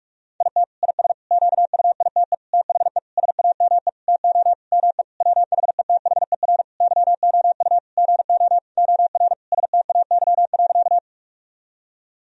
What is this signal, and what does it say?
Morse code, heard as a keyed tone, about 46 wpm, 700 Hz. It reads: IT IS QUITE THE SAME TO ME WHETHER YOU GO OR STAY.
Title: Sentences from Top 1000 Words - Review Part 3 - 45wpm